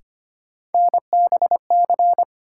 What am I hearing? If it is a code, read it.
Morse code, heard as a keyed tone, about 25 wpm, 700 Hz. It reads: NBC
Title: Sets of 5 Words Encore - New Word Drill - 25wpm